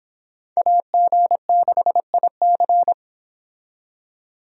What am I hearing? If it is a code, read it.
Morse code, heard as a keyed tone, about 30 wpm, 700 Hz. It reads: AG6IC